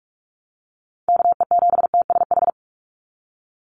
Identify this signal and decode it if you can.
Morse code, heard as a keyed tone, about 44 wpm, 700 Hz. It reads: KE7TSH
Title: USA Call Signs 45wpm